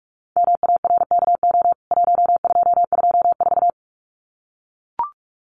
Morse code, heard as a keyed tone, about 44 wpm, 700 Hz. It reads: MARKO 1224
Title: CWT Exchange 45wpm